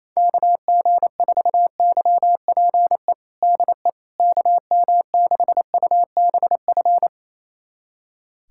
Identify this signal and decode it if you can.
Morse code, heard as a keyed tone, about 27 wpm, 700 Hz. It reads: KG4YPE DE KM6UBF
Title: USA Callsign DE Callsign 28wpm